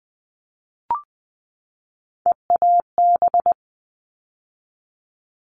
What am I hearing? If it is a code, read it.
Morse code, heard as a keyed tone, about 20 wpm, 700 Hz. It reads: EAB
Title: All Cut Numbers - 3-Digit 20wpm